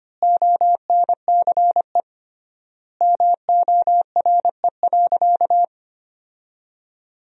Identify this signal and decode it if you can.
Morse code, heard as a keyed tone, about 25 wpm, 700 Hz. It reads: ONCE MORE.
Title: 3x Word-Spacing - Sentences from Top 500 Words - 25wpm